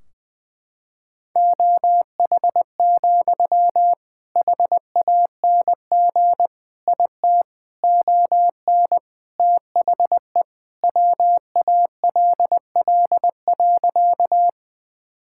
Morse code, heard as a keyed tone, about 20 wpm, 700 Hz. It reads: OH, HANG IT ON THE WALL.
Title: Sentences from Top 1600 Words 20wpm